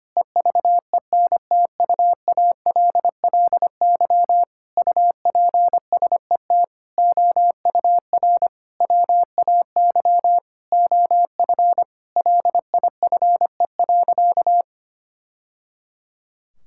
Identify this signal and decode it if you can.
Morse code, heard as a keyed tone, about 25 wpm, 700 Hz. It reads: EVENTUALLY UPSET OUR WAY OF LIFE.
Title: Sentences from Top 1000 Words - Review Part 6 - 25wpm